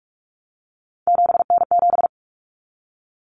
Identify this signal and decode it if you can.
Morse code, heard as a keyed tone, about 44 wpm, 700 Hz. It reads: 7N7
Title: Common Cut Numbers - 3-Digit 45wpm